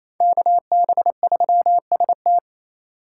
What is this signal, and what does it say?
Morse code, heard as a keyed tone, about 35 wpm, 700 Hz. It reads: KB3ST
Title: Speed Racing - USA 2x2 Call Signs 28wpm